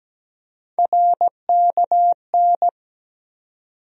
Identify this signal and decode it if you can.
Morse code, heard as a keyed tone, about 17 wpm, 700 Hz. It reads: RKN
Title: USA Call Sign Postfixes - 3 Letters 17wpm